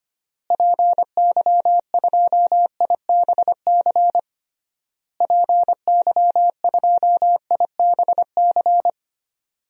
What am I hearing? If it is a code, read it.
Morse code, heard as a keyed tone, about 25 wpm, 700 Hz. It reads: PY2IBC PY2IBC
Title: DXCC-100 Call Signs - 6 Characters - Repeated 3x - 25wpm